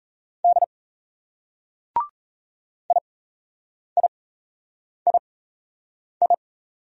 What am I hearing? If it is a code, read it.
Morse code, heard as a keyed tone, about 34 wpm, spaced as for 8 wpm, 700 Hz. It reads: D IIII
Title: Speed-Racing Single Letters 28wpm